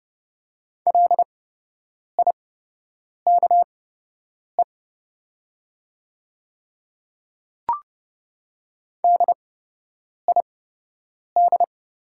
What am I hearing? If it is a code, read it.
Morse code, heard as a keyed tone, about 30 wpm, spaced as for 8 wpm, 700 Hz. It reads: LIKE DID
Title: 8x Character Spacing - Top 100 Words - 30wpm